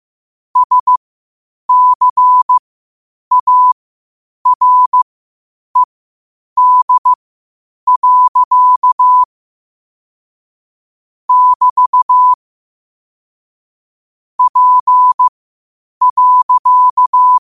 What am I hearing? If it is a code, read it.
Morse code, heard as a keyed tone, about 15 wpm, spaced as for 8 wpm, 1000 Hz. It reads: SCARED. <BT> P.
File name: qotdfiles / 8wpm